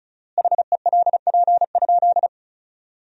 Morse code, heard as a keyed tone, about 34 wpm, 700 Hz. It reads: HELP?
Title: 4x Word-Spacing - Sentences from Top 200 Words - 35wpm